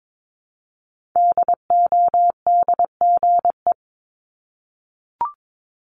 Morse code, heard as a keyed tone, about 22 wpm, 700 Hz. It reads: DODGE
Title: Multi-letter Phonemes Repeated 3x 22wpm